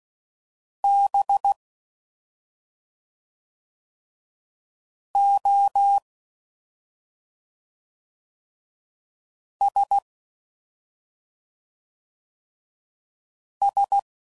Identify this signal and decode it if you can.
Morse code, heard as a keyed tone, about 16 wpm, 800 Hz. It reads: B O S S
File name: Suffixes / Special ING's / SlowToFaster ING's